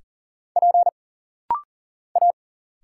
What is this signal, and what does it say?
Morse code, heard as a keyed tone, about 40 wpm, 700 Hz. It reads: P A
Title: Single Letters - Rapid-Fire 40wpm